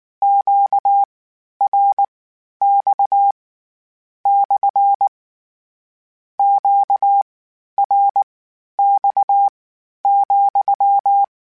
Morse code, heard as a keyed tone, about 19 wpm, spaced as for 11 wpm, 800 Hz. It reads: QRX / QRX,
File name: Q Calls / Q Calls-mp3s / Q Calls SPSL2 @18